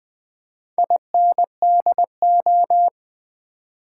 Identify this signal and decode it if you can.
Morse code, heard as a keyed tone, about 20 wpm, 700 Hz. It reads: INDO
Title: Build A Word - 2 Letters At A Time - 6-Letter Words - 20wpm